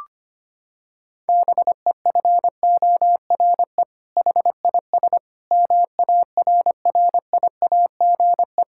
Morse code, heard as a keyed tone, about 25 wpm, 700 Hz. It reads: BEFORE HIS MARRIAGE
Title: Sentences from Top 1000 Words - Review Part 5 - 25wpm